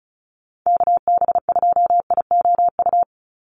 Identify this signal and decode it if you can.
Morse code, heard as a keyed tone, about 34 wpm, 700 Hz. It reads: KB2IOU
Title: USA Call Signs 35wpm